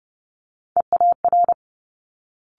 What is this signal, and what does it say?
Morse code, heard as a keyed tone, about 30 wpm, 700 Hz. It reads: EAR
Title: Multi-letter Phonemes Quick Fire 30wpm